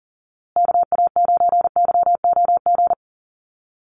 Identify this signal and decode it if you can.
Morse code, heard as a keyed tone, about 40 wpm, 700 Hz. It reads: KA9YOG
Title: USA Call Signs Repeated 2x 40wpm